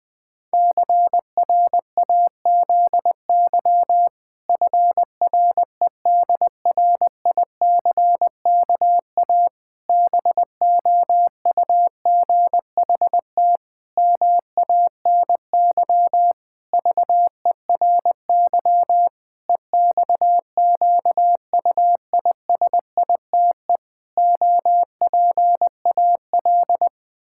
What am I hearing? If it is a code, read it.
Morse code, heard as a keyed tone, about 21 wpm, 700 Hz. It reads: CRAZY FREDRICKA BOUGHT MANY VERY EXQUISITE OPAL
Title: Pangram Challenge v2 - 20wpm